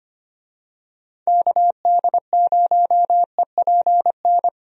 Morse code, heard as a keyed tone, about 25 wpm, 700 Hz. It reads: KD0EPN
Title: USA Call Signs - 6 Characters - 25wpm